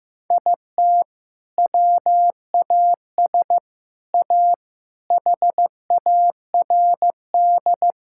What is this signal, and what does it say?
Morse code, heard as a keyed tone, about 15 wpm, 700 Hz. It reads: IT WAS A HARD
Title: Sentences from Top 1000 Words - Review Part 6 - 15wpm